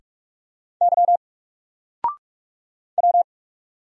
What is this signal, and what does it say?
Morse code, heard as a keyed tone, about 44 wpm, 700 Hz. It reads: Y W
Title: Single Letter-Number Alt 45wpm